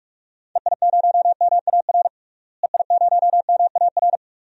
Morse code, heard as a keyed tone, about 46 wpm, 700 Hz. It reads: EI0MAR EI0MAR
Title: DXCC-100 Call Signs - 6 Characters - Repeated 3x - 45wpm